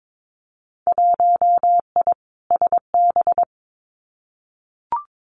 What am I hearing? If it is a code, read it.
Morse code, heard as a keyed tone, about 22 wpm, 700 Hz. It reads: 1I SB
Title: Winter Field Day Exchanges 22wpm